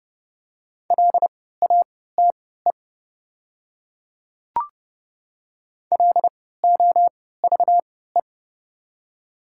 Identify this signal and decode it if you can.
Morse code, heard as a keyed tone, about 30 wpm, spaced as for 17 wpm, 690 Hz. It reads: LATE LOVE
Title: 3x Character Spacing - Top 500 Words - 30wpm